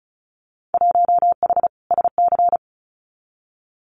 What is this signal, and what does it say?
Morse code, heard as a keyed tone, about 34 wpm, 700 Hz. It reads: 1H SC
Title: Winter Field Day Exchanges 35wpm